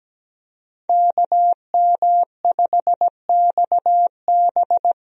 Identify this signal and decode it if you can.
Morse code, heard as a keyed tone, about 17 wpm, 700 Hz. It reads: KM5XB
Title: USA Call Signs - 5 Characters - Repeated 3x - 17wpm